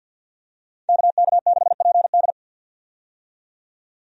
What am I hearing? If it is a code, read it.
Morse code, heard as a keyed tone, about 50 wpm, 700 Hz. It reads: KK6PD
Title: USA Call Signs - 5 Characters - 50wpm